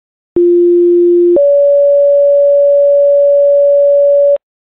Two-tone page, heard as a one-second tone, then a three-second tone, about 350 then 570 Hz, with noise. It is heard from 0.4 to 4.4 s.
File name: ringtones / boalsburg-fire-tones